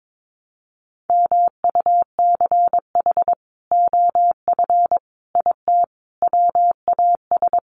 Morse code, heard as a keyed tone, about 22 wpm, 700 Hz. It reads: MUCH OF IT WAS
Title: Sentences from Top 400 Words 22wpm